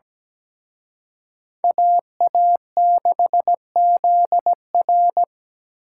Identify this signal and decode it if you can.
Morse code, heard as a keyed tone, about 20 wpm, 700 Hz. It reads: AA6ZR
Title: Speed Racing - USA Call Signs 17wpm